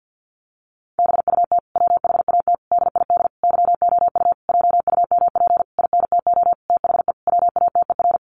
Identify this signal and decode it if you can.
Morse code, heard as a keyed tone, about 50 wpm, 700 Hz. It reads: BUT WHAT DID YOU JUMP INTO THE WATER
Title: Sentences from Top 1300 Words 50wpm